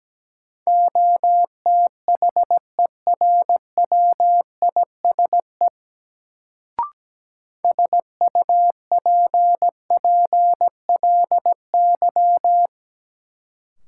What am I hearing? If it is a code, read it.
Morse code, heard as a keyed tone, about 17 wpm, 700 Hz. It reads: OTHERWISE SUPPLY
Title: Sentences from Top 1200 Words - New Word Drill Alt - 17wpm